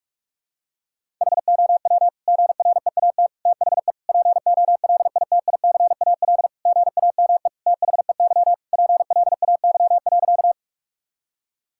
Morse code, heard as a keyed tone, about 41 wpm, 700 Hz. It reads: HOW GREAT THE POLITICAL GAME THEY PLAY.
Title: Sentences from Top 1000 Words - Review Part 5 - 45wpm